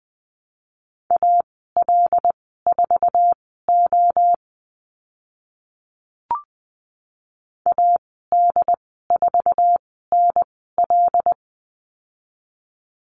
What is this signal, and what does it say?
Morse code, heard as a keyed tone, about 20 wpm, spaced as for 15 wpm, 700 Hz. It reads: AL4O AD4NL
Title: 076 Lesson 13g - Letter L - Call Signs 20wpm